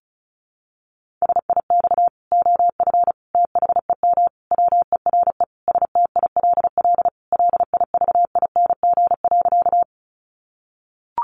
Morse code, heard as a keyed tone, about 34 wpm, 700 Hz. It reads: SIX OF THEM WERE STILL LIVING.
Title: Sentences from Top 1100 Words 35wpm